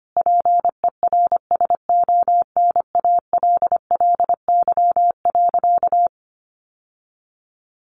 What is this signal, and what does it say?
Morse code, heard as a keyed tone, about 25 wpm, 700 Hz. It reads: PERSONALLY.
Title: Sentences from Top 1400 Words 25wpm